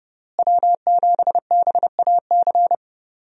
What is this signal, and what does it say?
Morse code, heard as a keyed tone, about 30 wpm, 700 Hz. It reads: W7BAC
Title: USA Call Signs - 5 Characters - Repeated 3x - 30wpm